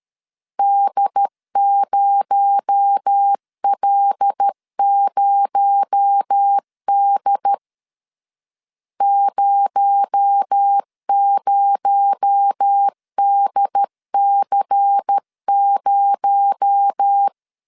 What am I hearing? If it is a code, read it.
Morse code, heard as a keyed tone, about 13 wpm, 800 Hz. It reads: D0L0D 00DC0
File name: LEZIONI / 6 PKX0 / 6 PKX0 12wpm 2231